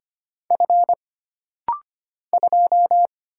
Morse code, heard as a keyed tone, about 25 wpm, 700 Hz. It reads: F 2
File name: Single Character (Letter:Number:Punctuation:Prosign) - Rapid-Fire - 25wpm